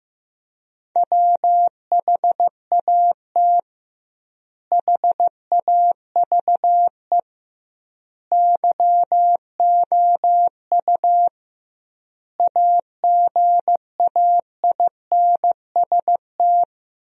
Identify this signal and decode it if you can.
Morse code, heard as a keyed tone, about 15 wpm, 700 Hz. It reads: WHAT HAVE YOU AGAINST
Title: 2x Word-Spacing - Sentences from Top 400 Words - 15wpm